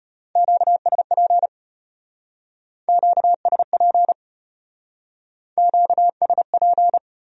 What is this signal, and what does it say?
Morse code, heard as a keyed tone, about 33 wpm, 700 Hz. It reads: QSP QSP QSP